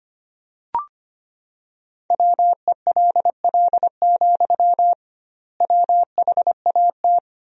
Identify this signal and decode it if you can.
Morse code, heard as a keyed tone, about 25 wpm, 700 Hz. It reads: WELL, WHAT
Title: 2x Word-Spacing - Sentences from Top 200 Words - 25wpm